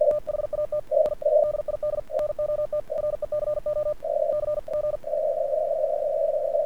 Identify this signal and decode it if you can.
Morse code, heard as a keyed tone, about 48 wpm, spaced as for 41 wpm, 1200 Hz, with noise. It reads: THAT I DID NOT RECOCG
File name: archived / CRSnet